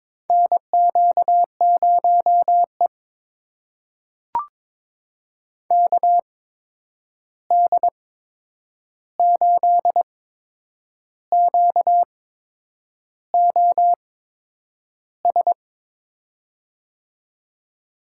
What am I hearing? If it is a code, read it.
Morse code, heard as a keyed tone, about 22 wpm, spaced as for 8 wpm, 700 Hz. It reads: NQ0E KD8QOS
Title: USA Call Signs - 8x Character Spacing - 22wpm